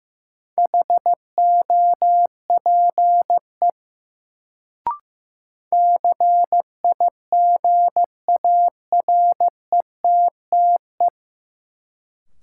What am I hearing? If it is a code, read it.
Morse code, heard as a keyed tone, about 15 wpm, 700 Hz. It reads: HOPE CIGARETTE
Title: Sentences from Top 1000 Words - New Word Drill - 15wpm